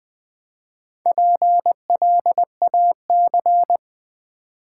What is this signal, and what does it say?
Morse code, heard as a keyed tone, about 20 wpm, 700 Hz. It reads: PLAC
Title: Build A Word - 1 Letter At A Time - 3 to 5-Letter Words - 20wpm